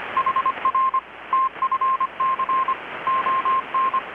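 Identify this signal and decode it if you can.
Morse code, heard as a keyed tone, about 25 wpm, 1060 Hz, with radio noise. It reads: HR TFC ON
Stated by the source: Morse from KPH's Night of Nights XX maritime radiotelegraphy event, received in Chiapas, Mexico about 3,800 km away on a Kenwood R-600 with a grounded 10 m long-wire antenna.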